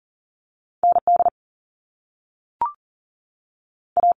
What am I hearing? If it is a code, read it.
Morse code, heard as a keyed tone, about 41 wpm, 700 Hz. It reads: ND A
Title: Top Two Letter Combinations in Words 40wpm